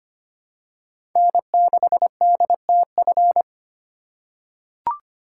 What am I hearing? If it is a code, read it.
Morse code, heard as a keyed tone, about 25 wpm, 700 Hz. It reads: N6DTF
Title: USA Call Signs - 5 Characters - 4x Character Spacing - 25wpm